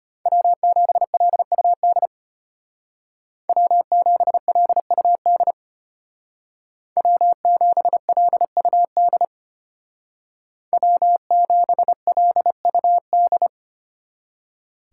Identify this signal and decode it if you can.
Morse code, heard as a keyed tone, about 33 wpm, 700 Hz. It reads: W7LUD W7LUD W7LUD W7LUD